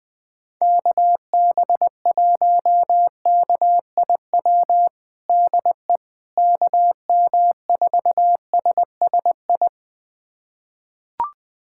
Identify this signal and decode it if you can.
Morse code, heard as a keyed tone, about 20 wpm, 700 Hz. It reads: KB1KIW DE KM4SSI
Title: USA Callsign DE Callsign 20wpm